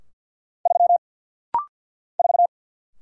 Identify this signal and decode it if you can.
Morse code, heard as a keyed tone, about 50 wpm, 690 Hz. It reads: 3 4
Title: Single Letter-Number - Rapid-Fire 50wpm